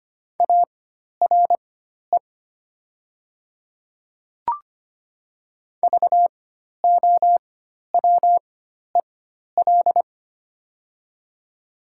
Morse code, heard as a keyed tone, about 25 wpm, spaced as for 12 wpm, 700 Hz. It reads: ARE VOWEL
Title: 4x Character Spacing - Top 500 Words - 25wpm